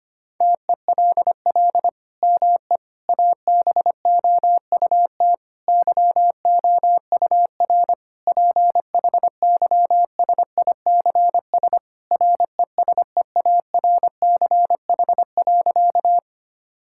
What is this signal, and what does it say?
Morse code, heard as a keyed tone, about 25 wpm, 700 Hz. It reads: TELL ME ABOUT YOUR PHYSICS RESEARCH.
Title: Sentences from Top 1000 Words - Review Part 5 - 25wpm